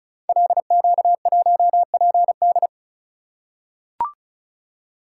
Morse code, heard as a keyed tone, about 36 wpm, 700 Hz. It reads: LQ1PD
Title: Code Groups - 5 Letter:Numbers - 35wpm